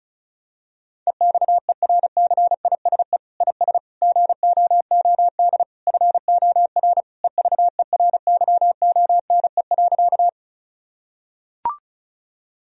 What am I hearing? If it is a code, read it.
Morse code, heard as a keyed tone, about 34 wpm, 700 Hz. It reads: EXERCISE IS GOOD FOR EVERYONE.
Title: Sentences from Top 1000 Words - Review Part 4 - 35wpm